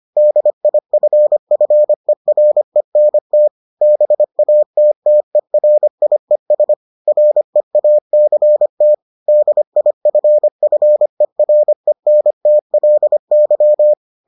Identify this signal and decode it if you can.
Morse code, heard as a keyed tone, about 25 wpm, 600 Hz. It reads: DIFFERENT BATTERIES REACT DIFFERENTLY